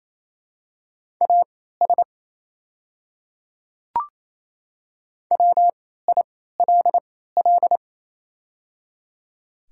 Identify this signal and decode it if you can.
Morse code, heard as a keyed tone, about 28 wpm, spaced as for 16 wpm, 700 Hz. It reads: AS WILL
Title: 3x Character Spacing - Top 100 Words - 28wpm